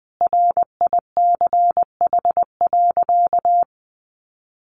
Morse code, heard as a keyed tone, about 20 wpm, 700 Hz. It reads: RICH.
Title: Sentences from Top 1100 Words 20wpm